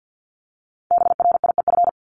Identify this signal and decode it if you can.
Morse code, heard as a keyed tone, about 50 wpm, 700 Hz. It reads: BRIEF
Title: Multi-letter Phonemes Repeated 3x 50wpm